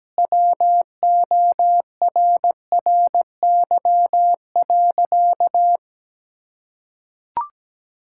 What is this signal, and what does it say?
Morse code, heard as a keyed tone, about 17 wpm, 700 Hz. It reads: WORRY.
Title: Pangram Challenge v2 - 17wpm